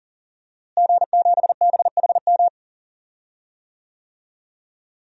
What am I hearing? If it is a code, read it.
Morse code, heard as a keyed tone, about 39 wpm, 700 Hz. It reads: G7BHM
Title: DXCC-100 Call Signs - 5 Characters - Repeated 3x - 40wpm